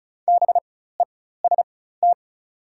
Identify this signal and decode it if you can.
Morse code, heard as a keyed tone, about 36 wpm, spaced as for 16 wpm, 700 Hz. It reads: BEST
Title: 4x Character Spacing - Top 500 Words - 35wpm